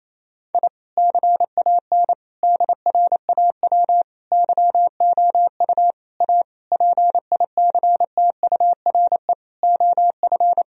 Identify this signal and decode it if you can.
Morse code, heard as a keyed tone, about 28 wpm, 700 Hz. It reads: I CAN DRAW YOU A PICTURE OF
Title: Sentences from Top 1000 Words - Review Part 1 - 28wpm